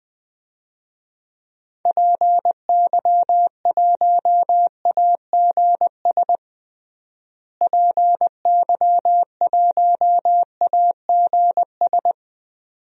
Morse code, heard as a keyed tone, about 21 wpm, 700 Hz. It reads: PY1AGS PY1AGS
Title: DXCC-100 Call Signs - 6 Characters - Repeated 2x - 20wpm